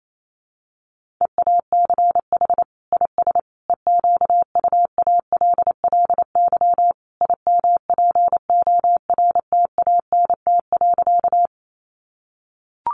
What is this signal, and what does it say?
Morse code, heard as a keyed tone, about 28 wpm, 700 Hz. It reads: EACH IS EQUALLY IMPORTANT.
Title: Sentences from Top 1400 Words 28wpm